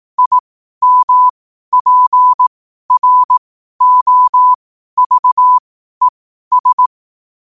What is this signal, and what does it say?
Morse code, heard as a keyed tone, about 18 wpm, spaced as for 12 wpm, 1000 Hz. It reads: IMPROVES
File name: qotdfiles / 12wpm